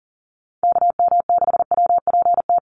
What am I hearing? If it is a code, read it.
Morse code, heard as a keyed tone, about 40 wpm, 700 Hz. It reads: KM6WPT
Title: USA Call Signs Repeated 3x 40wpm